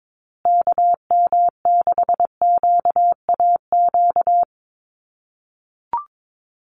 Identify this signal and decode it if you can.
Morse code, heard as a keyed tone, about 22 wpm, 700 Hz. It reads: KM6QAQ
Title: USA Call Signs Repeated 3x 22wpm